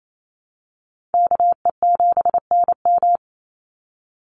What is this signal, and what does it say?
Morse code, heard as a keyed tone, about 28 wpm, 700 Hz. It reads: KE7NM
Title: USA Call Signs 28wpm